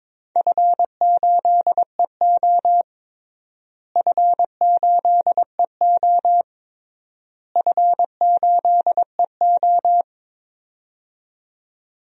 Morse code, heard as a keyed tone, about 22 wpm, 700 Hz. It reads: F8EO F8EO F8EO
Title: DXCC-100 Call Signs - 4 Characters - Repeated 3x - 22wpm